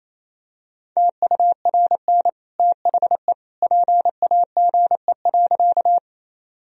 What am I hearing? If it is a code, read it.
Morse code, heard as a keyed tone, about 28 wpm, 700 Hz. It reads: TURN THE PAGE.
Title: Sentences from Top 1000 Words - Review Part 1 - 28wpm